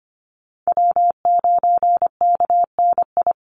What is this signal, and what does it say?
Morse code, heard as a keyed tone, about 25 wpm, 700 Hz. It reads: W9KNI
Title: Call Signs Repeated 3x - 25wpm